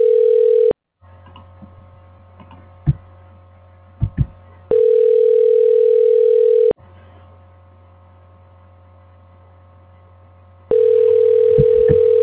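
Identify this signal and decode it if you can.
Ringback tone (North America), 440 + 480 Hz.
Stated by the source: Outbound call to Lowe's Home Improvement in Burbank, not answered, forwarded to an automated voice messaging system.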